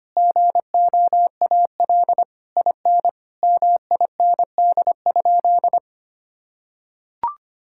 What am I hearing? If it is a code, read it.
Morse code, heard as a keyed tone, about 25 wpm, 700 Hz. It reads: GOAL IN MIND?
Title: Sentences from Top 1000 Words - Review Part 5 - 25wpm